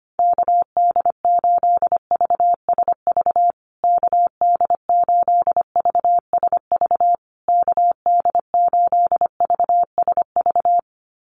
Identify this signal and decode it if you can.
Morse code, heard as a keyed tone, about 25 wpm, 700 Hz. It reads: KD8VSV KD8VSV KD8VSV
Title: USA Call Signs Repeated 3x 25wpm